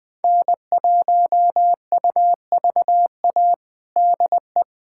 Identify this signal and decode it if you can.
Morse code, heard as a keyed tone, about 20 wpm, 700 Hz. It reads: N1UVA DE
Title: USA Callsign DE Callsign - 5 Characters - 20wpm